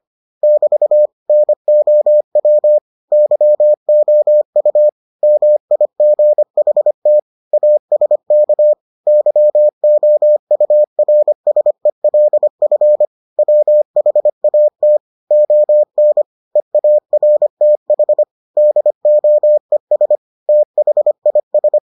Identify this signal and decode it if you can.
Morse code, heard as a keyed tone, about 25 wpm, 600 Hz. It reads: <BT> NOW YOU MIGHT ASK YOURSELF WHAT ON EARTH DOES THIS